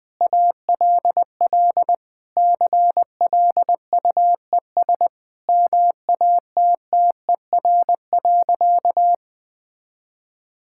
Morse code, heard as a keyed tone, about 20 wpm, 700 Hz. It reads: ALL CLUES MATTER.
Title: Sentences with 4-Word Phrases - 20wpm